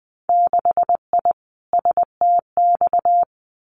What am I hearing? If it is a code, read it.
Morse code, heard as a keyed tone, about 20 wpm, 700 Hz. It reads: 6I STX
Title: Winter Field Day Exchanges Encore 20wpm